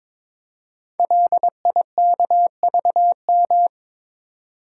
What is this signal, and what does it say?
Morse code, heard as a keyed tone, about 22 wpm, 700 Hz. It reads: LIKVM
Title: Code Groups - 5 Letters - 22wpm